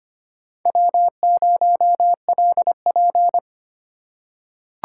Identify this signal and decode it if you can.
Morse code, heard as a keyed tone, about 25 wpm, 700 Hz. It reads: W0LP
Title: USA Call Signs - 4 Characters - 4x Character Spacing - 25wpm